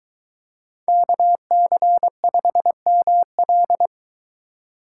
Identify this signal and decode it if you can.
Morse code, heard as a keyed tone, about 22 wpm, 700 Hz. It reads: KC5ML